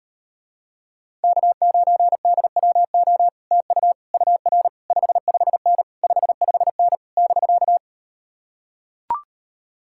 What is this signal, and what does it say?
Morse code, heard as a keyed tone, about 38 wpm, 700 Hz. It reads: K9DWO TU UR 55N 55N <BK>